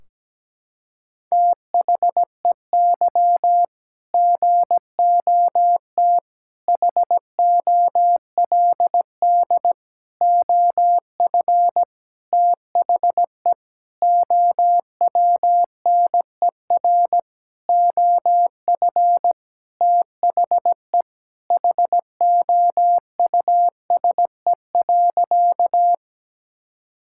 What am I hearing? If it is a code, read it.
Morse code, heard as a keyed tone, about 17 wpm, 700 Hz. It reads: THEY GOT HOLD OF THE OWNER OF THE HOUSE.
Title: Sentences from Top 1600 Words 17wpm